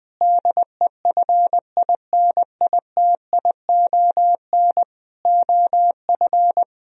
Morse code, heard as a keyed tone, about 21 wpm, 700 Hz. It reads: DEFINITION OF
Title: Sentences from Top 1000 Words - Review Part 4 - 20wpm